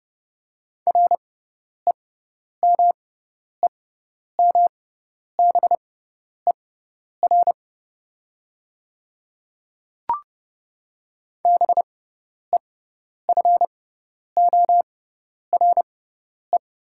Morse code, heard as a keyed tone, about 30 wpm, spaced as for 10 wpm, 700 Hz. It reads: REMEMBER BEFORE
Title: 6x Character Spacing - Top 500 Words - 30wpm